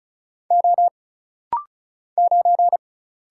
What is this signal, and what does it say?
Morse code, heard as a keyed tone, about 36 wpm, 700 Hz. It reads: O 9
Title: Single Letter-Number - Rapid-Fire 35wpm